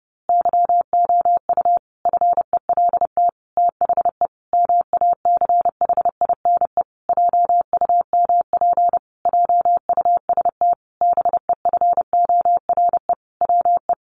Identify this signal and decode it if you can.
Morse code, heard as a keyed tone, about 28 wpm, 700 Hz. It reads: YOU FELT THE MACHINE JUMP JUST BEFORE WE
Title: A Sound Of Thunder 30wpm